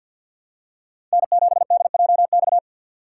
Morse code, heard as a keyed tone, about 51 wpm, 700 Hz. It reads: N7DJX
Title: USA Call Signs - 5 Characters - 4x Character Spacing - 50wpm